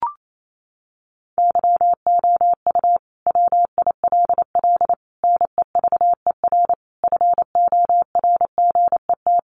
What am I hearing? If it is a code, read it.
Morse code, heard as a keyed tone, about 28 wpm, 700 Hz. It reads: YOU WILL NEVER FORGET
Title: Sentences from Top 1200 Words 28wpm